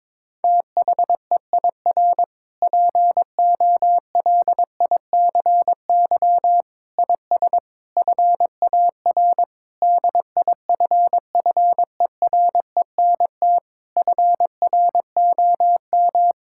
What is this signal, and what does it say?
Morse code, heard as a keyed tone, about 22 wpm, 700 Hz. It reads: THEIR POLICY IS FAR DIFFERENT FROM
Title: Sentences from Top 1000 Words - Review Part 5 - 22wpm